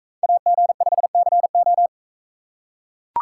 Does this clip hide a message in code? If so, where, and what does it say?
Yes: Morse code from 0.2 to 1.9 s, heard as a keyed tone, about 40 wpm, 700 Hz. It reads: AG5CO